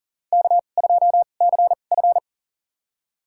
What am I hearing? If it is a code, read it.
Morse code, heard as a keyed tone, about 40 wpm, spaced as for 29 wpm, 700 Hz. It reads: K2CF
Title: USA Call Signs - 4 Characters - 2x Character Spacing - 40wpm